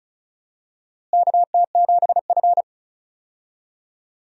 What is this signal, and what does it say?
Morse code, heard as a keyed tone, about 34 wpm, 700 Hz. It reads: KT7F
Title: USA Call Signs - 4 Characters - Repeated 2x - 35wpm